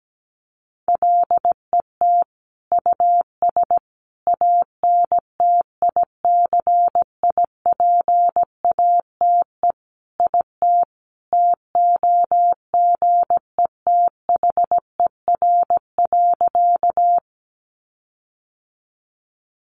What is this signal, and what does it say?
Morse code, heard as a keyed tone, about 17 wpm, 700 Hz. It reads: LET US ANTICIPATE IT TOGETHER.
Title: Sentences from Top 1500 Words 17wpm